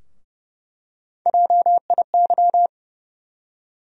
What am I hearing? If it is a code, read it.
Morse code, heard as a keyed tone, about 30 wpm, 700 Hz. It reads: JIY
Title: USA Call Sign Postfixes - 3 Letters 30wpm